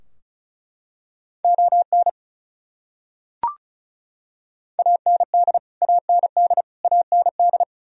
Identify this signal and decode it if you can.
Morse code, heard as a keyed tone, about 36 wpm, 700 Hz. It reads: ON AND AND AND
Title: Top 100 Words Repeated 35wpm